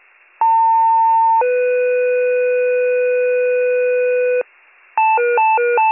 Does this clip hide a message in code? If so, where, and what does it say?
Yes: two-tone page from 0.4 to 4.4 s, heard as a one-second tone, then a three-second tone, about 900 then 500 Hz, with noise.